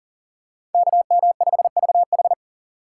Morse code, heard as a keyed tone, about 40 wpm, 700 Hz. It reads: KM5VH
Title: USA Call Signs - 5 Characters - 4x Character Spacing - 40wpm